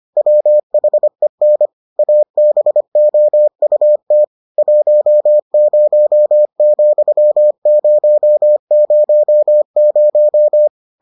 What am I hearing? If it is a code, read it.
Morse code, heard as a keyed tone, about 25 wpm, 600 Hz. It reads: WHEN ABOUT 10,000